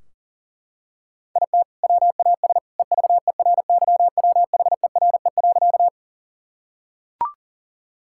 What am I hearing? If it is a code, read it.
Morse code, heard as a keyed tone, about 40 wpm, 700 Hz. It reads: IT WAS EVERYWHERE.
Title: Sentences from Top 1600 Words 40wpm